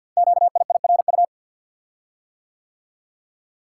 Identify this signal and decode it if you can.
Morse code, heard as a keyed tone, about 51 wpm, 700 Hz. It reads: QIIRU
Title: Code Groups - 5 Letters - 50wpm